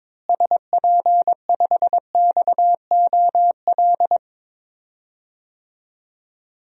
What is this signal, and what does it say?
Morse code, heard as a keyed tone, about 22 wpm, 700 Hz. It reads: SP5XOL
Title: DXCC-100 Call Signs - 6 Characters - Repeated 3x - 22wpm